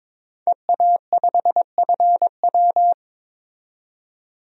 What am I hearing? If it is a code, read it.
Morse code, heard as a keyed tone, about 22 wpm, 700 Hz. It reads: EA5FW
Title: DXCC-100 Call Signs - 5 Characters - 22wpm